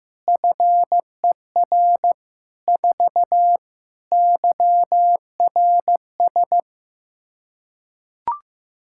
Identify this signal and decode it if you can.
Morse code, heard as a keyed tone, about 16 wpm, 700 Hz. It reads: FER 4 YRS
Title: Practice QSO - Exchange 3 - More Info - 15wpm